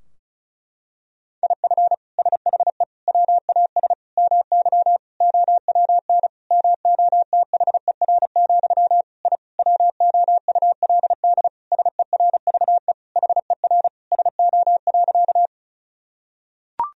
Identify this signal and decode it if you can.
Morse code, heard as a keyed tone, about 36 wpm, 700 Hz. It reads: IF SHE WAS MY OWN MOTHER, I WOULD SERVE HER SO.
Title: Sentences from Top 1000 Words - Review Part 2 - 35wpm